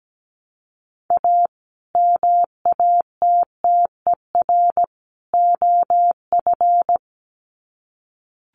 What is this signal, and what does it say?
Morse code, heard as a keyed tone, about 17 wpm, 700 Hz. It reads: A MATTER OF
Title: Sets of 3 Words 17wpm